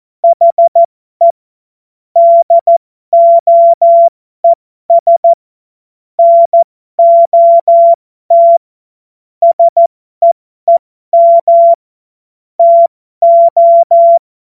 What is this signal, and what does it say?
Morse code, heard as a keyed tone, about 14 wpm, spaced as for 12 wpm, 680 Hz. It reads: HE DOES NOT SEEM TO
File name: Morsekurs / Witze